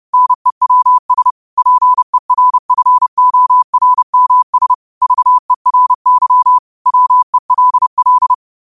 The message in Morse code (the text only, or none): NEWS PERFORMS VERY WELL